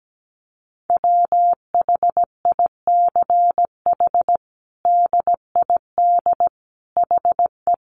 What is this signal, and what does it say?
Morse code, heard as a keyed tone, about 17 wpm, 700 Hz. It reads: WHICH DID HE
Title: Sentences from Top 1100 Words 17wpm